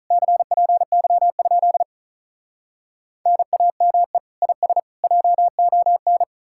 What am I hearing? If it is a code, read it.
Morse code, heard as a keyed tone, about 46 wpm, 700 Hz. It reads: CPY? NAME IS JON